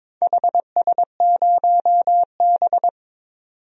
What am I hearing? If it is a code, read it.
Morse code, heard as a keyed tone, about 22 wpm, 700 Hz. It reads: HS0B
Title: DXCC-100 Call Signs - 4 Characters - 22wpm